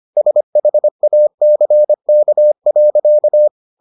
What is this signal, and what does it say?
Morse code, heard as a keyed tone, about 25 wpm, 600 Hz. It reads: SHACK.